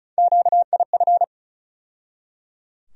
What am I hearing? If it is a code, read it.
Morse code, heard as a keyed tone, about 35 wpm, 700 Hz. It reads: QIF
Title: USA Call Sign Postfixes - 3 Letters 35wpm